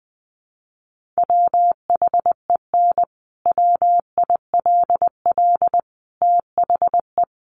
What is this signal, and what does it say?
Morse code, heard as a keyed tone, about 20 wpm, 700 Hz. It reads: WHEN WILL THE
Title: Sentences from Top 1400 Words 20wpm